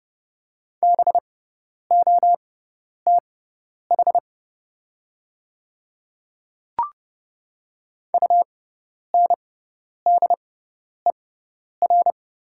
Morse code, heard as a keyed tone, about 31 wpm, spaced as for 10 wpm, 700 Hz. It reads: BOTH UNDER
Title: 6x Character Spacing - Top 300 Words - 30wpm